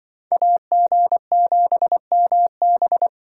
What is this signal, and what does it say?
Morse code, heard as a keyed tone, about 25 wpm, 700 Hz. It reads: AG7MB